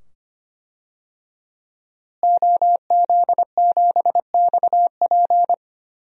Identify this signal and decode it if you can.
Morse code, heard as a keyed tone, about 25 wpm, 700 Hz. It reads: OZ7XP
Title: DXCC-100 Call Signs - 5 Characters - Repeated 2x - 25wpm